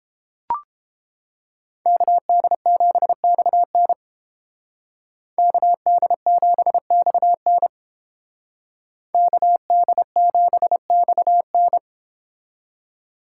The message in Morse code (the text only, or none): KD7XN KD7XN KD7XN